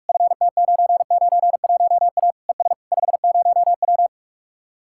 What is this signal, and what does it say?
Morse code, heard as a keyed tone, about 49 wpm, 700 Hz. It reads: FT991A ES 50W